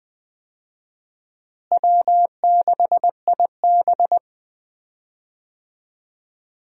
Morse code, heard as a keyed tone, about 21 wpm, 700 Hz. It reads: W6IB
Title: USA Call Signs - 4 Characters - Repeated 3x - 20wpm